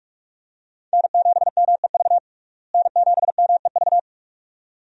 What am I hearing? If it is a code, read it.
Morse code, heard as a keyed tone, about 48 wpm, 700 Hz. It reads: N7MEV N7MEV
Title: USA Call Signs - 5 Characters - Repeated 2x - 45wpm